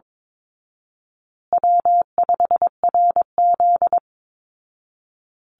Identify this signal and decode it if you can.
Morse code, heard as a keyed tone, about 22 wpm, 700 Hz. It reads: W5RZ
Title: Easy Call Signs Repeated 3x - 22wpm